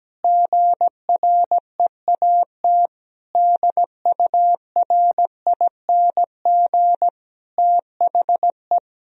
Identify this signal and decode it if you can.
Morse code, heard as a keyed tone, about 17 wpm, 700 Hz. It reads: GREAT DURING THE
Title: Sentences from Top 1000 Words - Review Part 3 - 17wpm